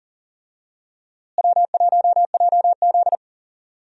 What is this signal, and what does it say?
Morse code, heard as a keyed tone, about 40 wpm, 700 Hz. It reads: W1JZ
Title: USA Call Signs - 4 Characters - Repeated 3x - 40wpm